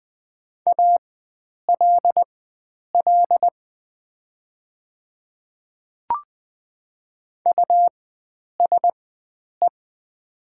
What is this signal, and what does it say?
Morse code, heard as a keyed tone, about 20 wpm, spaced as for 9 wpm, 700 Hz. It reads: ALL USE
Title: 4x Character Spacing - Top 100 Words - 20wpm